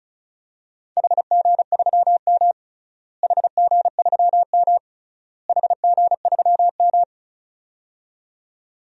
Morse code, heard as a keyed tone, about 36 wpm, 700 Hz. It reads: HG3M HG3M HG3M
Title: DXCC-100 Call Signs - 4 Characters - Repeated 3x - 35wpm